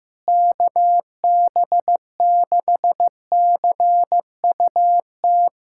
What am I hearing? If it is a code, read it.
Morse code, heard as a keyed tone, about 16 wpm, 700 Hz. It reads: KB6CUT
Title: USA Callsign DE Callsign - 6 Characters - 15wpm